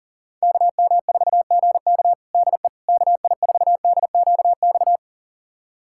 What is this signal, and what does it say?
Morse code, heard as a keyed tone, about 40 wpm, 700 Hz. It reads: KM4GK DE KI4DQX
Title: USA Callsign DE Callsign 40wpm